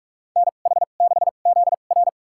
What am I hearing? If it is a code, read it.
Morse code, heard as a keyed tone, about 46 wpm, spaced as for 32 wpm, 700 Hz. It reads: NH6ZR
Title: USA Call Signs - 5 Characters - 2x Character Spacing - 45wpm